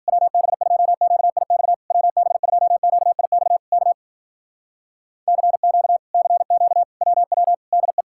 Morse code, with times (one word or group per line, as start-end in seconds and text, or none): 0.1-1.7 WB2QIX
1.9-3.6 WB2QIX
3.7-3.9 K
5.3-6.0 CQ
6.1-6.8 CQ
7.0-7.5 WW
7.7-8.0 DE